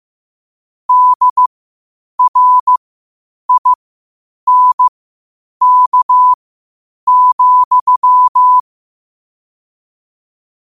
Morse code, heard as a keyed tone, about 15 wpm, spaced as for 8 wpm, 1000 Hz. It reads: DRINK,
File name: 8wpm